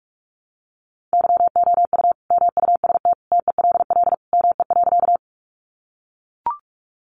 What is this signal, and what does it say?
Morse code, heard as a keyed tone, about 44 wpm, 700 Hz. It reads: YOU MUST TELL ME.
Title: Sentences from Top 200 Words 45wpm